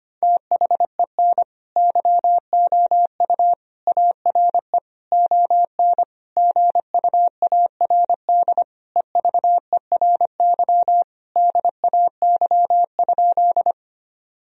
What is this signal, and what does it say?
Morse code, heard as a keyed tone, about 25 wpm, 700 Hz. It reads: THEN YOU ARE ON GUARD EVERY DAY?
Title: Sentences from Top 1000 Words - Review Part 4 - 25wpm